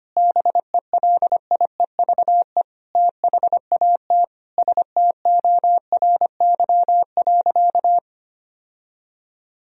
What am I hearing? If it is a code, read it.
Morse code, heard as a keyed tone, about 25 wpm, 700 Hz. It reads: BELIEVE THAT STORY.
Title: Sentences from Top 1000 Words - Review Part 1 - 25wpm